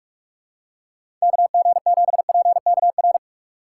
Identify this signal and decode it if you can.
Morse code, heard as a keyed tone, about 46 wpm, 700 Hz. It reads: KG7PKR
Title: USA Call Signs - 6 Characters - 4x Character Spacing - 45wpm